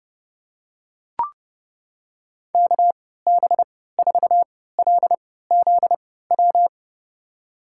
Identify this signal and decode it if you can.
Morse code, heard as a keyed tone, about 32 wpm, spaced as for 17 wpm, 700 Hz. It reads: KB4LZW
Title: USA Call Signs - 6 Characters - 3x Character Spacing - 30wpm